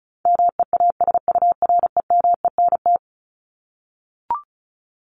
Morse code, heard as a keyed tone, about 34 wpm, 700 Hz. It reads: MEASUREMENT
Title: Top 1751-2000 Words 35wpm